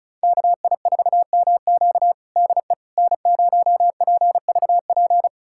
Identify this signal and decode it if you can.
Morse code, heard as a keyed tone, about 34 wpm, 700 Hz. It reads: KI4MQ DE N0PVP
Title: USA Callsign DE Callsign - 5 Characters - 35wpm